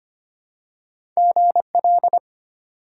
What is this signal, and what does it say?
Morse code, heard as a keyed tone, about 25 wpm, 700 Hz. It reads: GL
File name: Top 90 QSO Elements - 2x Character Spacing - 25wpm